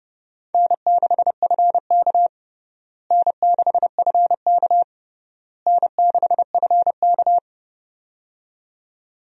Morse code, heard as a keyed tone, about 32 wpm, 700 Hz. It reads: N6FK N6FK N6FK
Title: USA Call Signs - 4 Characters - Repeated 3x - 30wpm